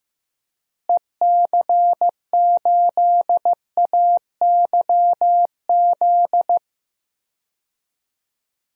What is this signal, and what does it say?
Morse code, heard as a keyed tone, about 16 wpm, 700 Hz. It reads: EC8AYZ
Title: DXCC-100 Call Signs - 6 Characters - Repeated 3x - 15wpm